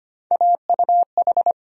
Morse code, heard as a keyed tone, about 25 wpm, 700 Hz. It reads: AUH
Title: USA Call Sign Postfixes - 3 Letters 25wpm